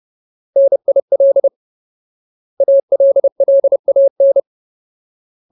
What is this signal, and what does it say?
Morse code, heard as a keyed tone, about 30 wpm, 550 Hz. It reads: NIL ALLAN
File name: Ultimate Word List / 30WPM Ultimate